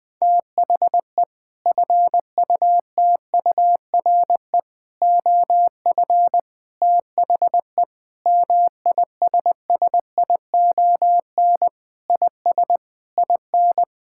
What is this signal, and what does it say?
Morse code, heard as a keyed tone, about 21 wpm, 700 Hz. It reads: THE FUTURE OF THE MISSION IS IN
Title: Sentences from Top 1600 Words 20wpm